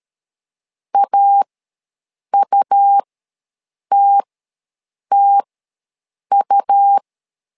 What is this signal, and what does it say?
Morse code, heard as a keyed tone, about 13 wpm, spaced as for 7 wpm, 800 Hz. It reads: AUTTU
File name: LEZIONI / 2 UHDN / 2 U ws 12wpm 1655